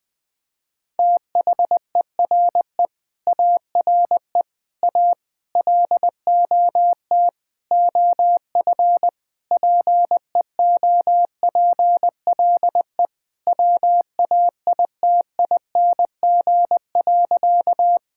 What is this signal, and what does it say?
Morse code, heard as a keyed tone, about 21 wpm, 700 Hz. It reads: THERE ARE A LOT OF PEOPLE WAITING.
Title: Sentences with 5-Word Phrases - 20wpm